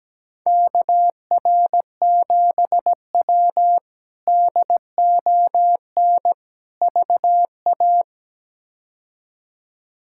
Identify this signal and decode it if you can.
Morse code, heard as a keyed tone, about 17 wpm, 700 Hz. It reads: KR7W DON VA
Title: K1USN SST QSO Practice - 17wpm